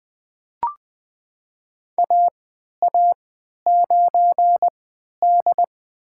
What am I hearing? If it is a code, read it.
Morse code, heard as a keyed tone, about 21 wpm, spaced as for 12 wpm, 700 Hz. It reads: AA9D
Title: USA Call Signs - 4 Characters - 3x Character Spacing - 20wpm